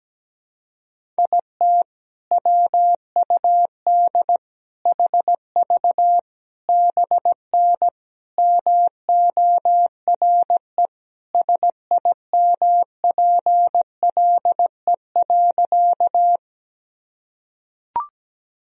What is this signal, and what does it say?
Morse code, heard as a keyed tone, about 17 wpm, 700 Hz. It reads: IT WUD HV BN MORE SIMPLE.
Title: Sentences from Top 400 Words and Abbreviations 17wpm